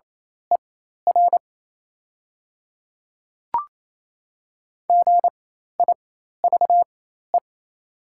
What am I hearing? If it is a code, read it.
Morse code, heard as a keyed tone, about 28 wpm, spaced as for 13 wpm, 700 Hz. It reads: ER GIVE
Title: 4x Character Spacing - Top 200 Words - 28wpm